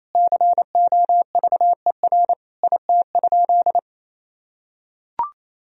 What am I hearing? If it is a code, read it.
Morse code, heard as a keyed tone, about 28 wpm, 700 Hz. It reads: COVER IT?
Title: Sentences from Top 1000 Words - Review Part 1 - 28wpm